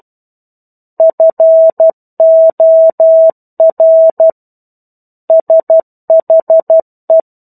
Morse code, heard as a keyed tone, about 12 wpm, 650 Hz. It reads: FOR SHE
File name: 01-Chapter-01 12 WPM